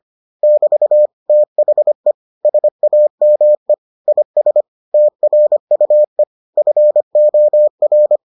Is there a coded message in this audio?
Yes: Morse code from 0.4 to 8.2 s, heard as a keyed tone, about 25 wpm, 600 Hz. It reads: <BT> THE SAME IS TRUE FOR